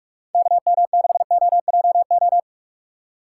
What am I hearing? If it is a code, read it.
Morse code, heard as a keyed tone, about 48 wpm, 700 Hz. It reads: KM6OJO